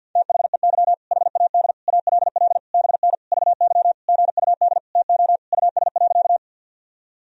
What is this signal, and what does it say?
Morse code, heard as a keyed tone, about 51 wpm, 700 Hz. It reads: THEY HAD ALL BN VY GUD TO US.
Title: Sentences from Top 200 Words and Abbreviations 50wpm